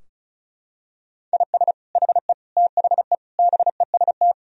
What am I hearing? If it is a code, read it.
Morse code, heard as a keyed tone, about 36 wpm, 700 Hz. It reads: IS HE THE BEST
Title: Sentences from Top 1000 Words - Review Part 4 - 35wpm